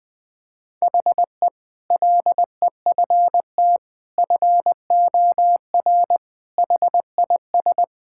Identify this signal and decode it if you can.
Morse code, heard as a keyed tone, about 19 wpm, 700 Hz. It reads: HE LEFT FOR HIS
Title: Sentences with 2-Word Phrases - 20wpm